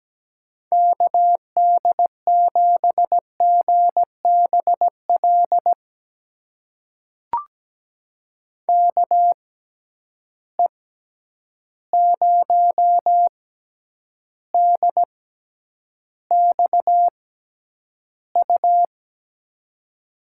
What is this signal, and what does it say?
Morse code, heard as a keyed tone, about 17 wpm, spaced as for 9 wpm, 700 Hz. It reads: KD7GBL KE0DXU
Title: USA Call Signs - 6 Characters - 6x Character Spacing - 17wpm